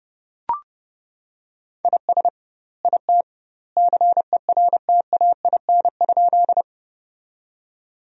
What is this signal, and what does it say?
Morse code, heard as a keyed tone, about 30 wpm, 700 Hz. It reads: IS IT CERTAIN?
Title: 2x Word-Spacing - Sentences from Top 500 Words - 30wpm